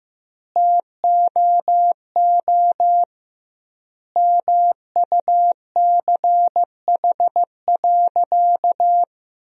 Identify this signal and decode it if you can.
Morse code, heard as a keyed tone, about 15 wpm, 700 Hz. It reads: TOO MUCH.
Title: 2x Word-Spacing - Sentences from Top 400 Words - 15wpm